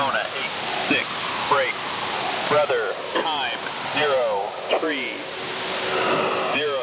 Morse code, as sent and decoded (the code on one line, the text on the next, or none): none